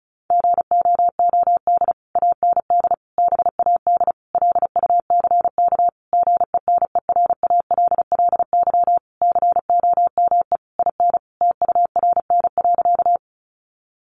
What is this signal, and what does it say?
Morse code, heard as a keyed tone, about 34 wpm, 700 Hz. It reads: GOOD AND BAD LUCK GENERALLY COME IN TURN.
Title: Sentences from Top 1500 Words 35wpm